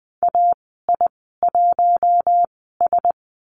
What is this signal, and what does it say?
Morse code, heard as a keyed tone, about 20 wpm, spaced as for 14 wpm, 700 Hz. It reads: AI1S
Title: 042 Lesson 8g - Number 1 - Call Signs 20wpm